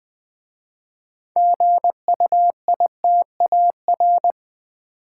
Morse code, heard as a keyed tone, about 20 wpm, 700 Hz. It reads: GUITAR
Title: Easy Words with Hints - 20wpm